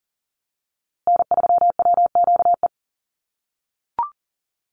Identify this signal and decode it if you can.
Morse code, heard as a keyed tone, about 40 wpm, 700 Hz. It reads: N3WQE
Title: USA Call Signs Repeated 2x 40wpm